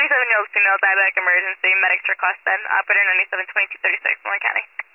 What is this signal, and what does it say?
Emergency Alert System (SAME) tones, heard as harsh buzzing bursts, with speech. It is heard from 1.2 to 4.7 s.